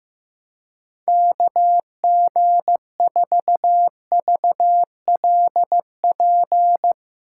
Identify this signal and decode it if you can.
Morse code, heard as a keyed tone, about 16 wpm, 700 Hz. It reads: KG4VLP
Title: USA Call Signs - 6 Characters - 3x Character Spacing - 15wpm